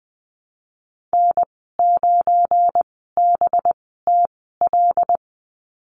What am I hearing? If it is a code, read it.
Morse code, heard as a keyed tone, about 20 wpm, spaced as for 15 wpm, 700 Hz. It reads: N9BTL
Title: 207 Lesson 33g - Letter j - Call Signs 20wpm